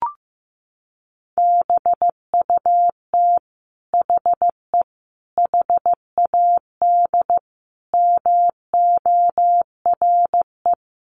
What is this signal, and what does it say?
Morse code, heard as a keyed tone, about 15 wpm, 700 Hz. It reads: BUT HE HAD MORE
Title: Sentences from Top 1500 Words 15wpm